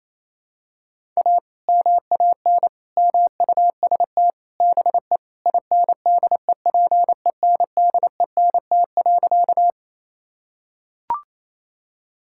Morse code, heard as a keyed tone, about 28 wpm, 700 Hz. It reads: A MAN MUST BE INDEPENDENT.
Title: Sentences from Top 1000 Words - Review Part 6 - 28wpm